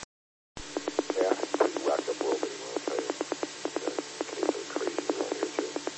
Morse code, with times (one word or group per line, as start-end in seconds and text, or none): none